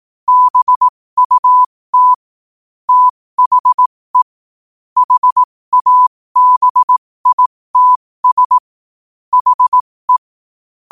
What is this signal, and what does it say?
Morse code, heard as a keyed tone, about 18 wpm, spaced as for 15 wpm, 1000 Hz. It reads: BUT THE HABITS HE
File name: qotdfiles / 15wpm